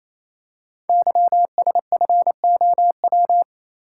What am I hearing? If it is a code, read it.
Morse code, heard as a keyed tone, about 29 wpm, 700 Hz. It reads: YSFOW
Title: Code Groups - 5 Letter:Numbers - 28wpm